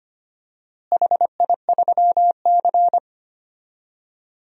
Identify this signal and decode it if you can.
Morse code, heard as a keyed tone, about 25 wpm, 700 Hz. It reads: HI3C
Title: DXCC-100 Call Signs - 4 Characters - Repeated 2x - 25wpm